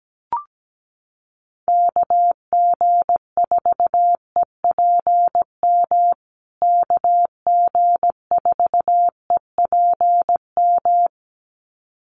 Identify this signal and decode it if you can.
Morse code, heard as a keyed tone, about 17 wpm, 700 Hz. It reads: KG4EPM KG4EPM
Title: USA Call Signs Repeated 2x 17wpm